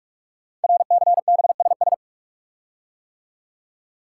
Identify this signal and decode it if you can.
Morse code, heard as a keyed tone, about 44 wpm, 700 Hz. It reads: RCBSS
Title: Code Groups - 5 Letters - 45wpm